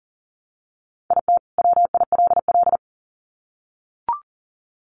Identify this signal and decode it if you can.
Morse code, heard as a keyed tone, about 39 wpm, 700 Hz. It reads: IT WILL
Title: Sets of 2 Words 40wpm